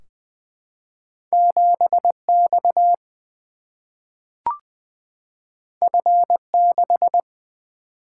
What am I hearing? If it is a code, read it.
Morse code, heard as a keyed tone, about 20 wpm, 700 Hz. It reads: 7 X F 6
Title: Single-Letter Single-Number 20wpm